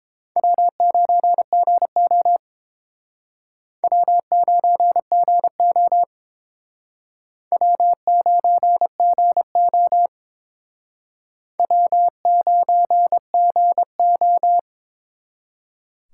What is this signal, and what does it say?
Morse code, heard as a keyed tone, about 29 wpm, 700 Hz. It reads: W9GO W9GO W9GO W9GO